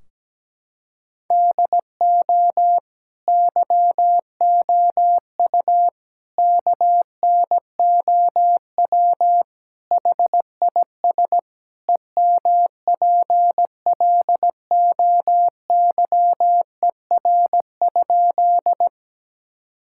Morse code, heard as a keyed tone, about 17 wpm, 700 Hz. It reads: DO YOU KNOW HIS EMPLOYER?
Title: Sentences from Top 1600 Words 17wpm